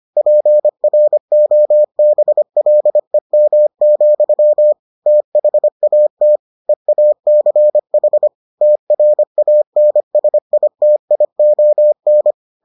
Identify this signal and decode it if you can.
Morse code, heard as a keyed tone, about 25 wpm, 600 Hz. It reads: PROBLEM, THAT EACH TRANSITION